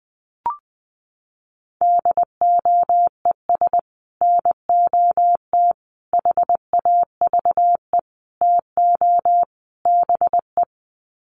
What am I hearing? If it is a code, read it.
Morse code, heard as a keyed tone, about 20 wpm, 700 Hz. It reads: DOES NOT HAVE TO BE
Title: Sets of 5 Words 20wpm